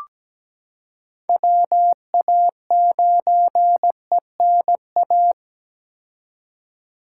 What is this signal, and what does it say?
Morse code, heard as a keyed tone, about 17 wpm, 700 Hz. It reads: WA9ENA
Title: K1USN SST QSO Practice - 17wpm